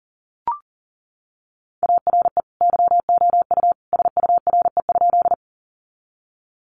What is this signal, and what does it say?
Morse code, heard as a keyed tone, about 40 wpm, 700 Hz. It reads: ARE YOU SURE?
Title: Sentences from Top 300 Words 40wpm